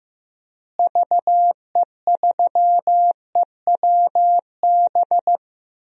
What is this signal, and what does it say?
Morse code, heard as a keyed tone, about 15 wpm, 700 Hz. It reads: VE3EWB
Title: DXCC-100 Call Signs - 6 Characters - Repeated 2x - 15wpm